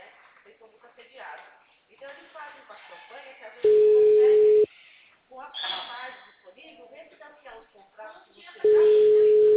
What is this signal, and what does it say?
Ringback tone (Europe, 425 Hz), with speech.